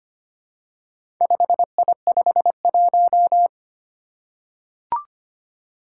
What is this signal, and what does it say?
Morse code, heard as a keyed tone, about 25 wpm, 700 Hz. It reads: 5I51
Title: Four Challenging Letter-Number Combinations 25wpm